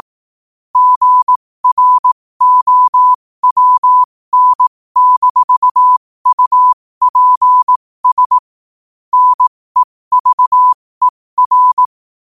Morse code, heard as a keyed tone, about 18 wpm, spaced as for 15 wpm, 1000 Hz. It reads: GROWN-UPS NEVER